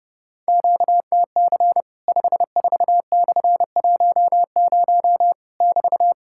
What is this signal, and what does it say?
Morse code, heard as a keyed tone, about 31 wpm, 700 Hz. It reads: QTC 54/10 <BT>
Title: Worked All Europe DX Contest - QTC Exchange - 30wpm